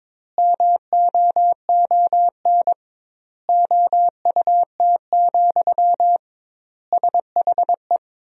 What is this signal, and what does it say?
Morse code, heard as a keyed tone, about 22 wpm, 700 Hz. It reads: MOON OUT, SHE
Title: 2x Word-Spacing - Sentences from Top 500 Words - 22wpm